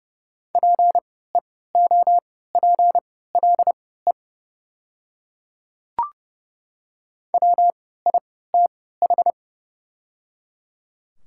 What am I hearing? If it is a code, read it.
Morse code, heard as a keyed tone, about 30 wpm, spaced as for 17 wpm, 700 Hz. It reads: PEOPLE WITH
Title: 3x Character Spacing - Top 100 Words - 30wpm